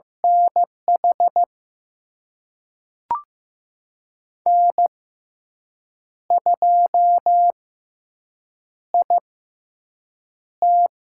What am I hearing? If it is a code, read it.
Morse code, heard as a keyed tone, about 15 wpm, spaced as for 6 wpm, 700 Hz. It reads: NH N2IT
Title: USA Call Signs - 5 Characters - 6x Character Spacing - 15wpm